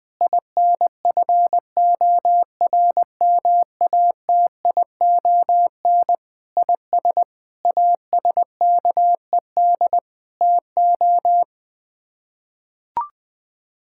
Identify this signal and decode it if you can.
Morse code, heard as a keyed tone, about 20 wpm, 700 Hz. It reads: INFORMATION IS ASKED TO
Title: Sets of 6 Words - 20wpm